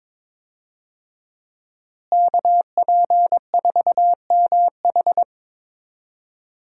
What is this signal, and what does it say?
Morse code, heard as a keyed tone, about 22 wpm, 700 Hz. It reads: KP4MH
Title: DXCC-100 Call Signs - 5 Characters - Repeated 2x - 22wpm